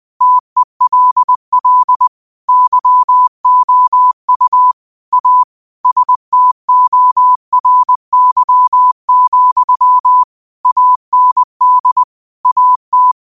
Morse code, heard as a keyed tone, about 20 wpm, 1000 Hz. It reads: TELL YOU A STORY, AND AT
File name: qotdfiles / 20wpm